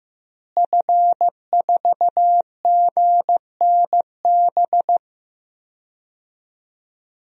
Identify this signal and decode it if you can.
Morse code, heard as a keyed tone, about 15 wpm, 700 Hz. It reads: F4GNB
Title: DXCC-100 Call Signs - 5 Characters - Repeated 2x - 15wpm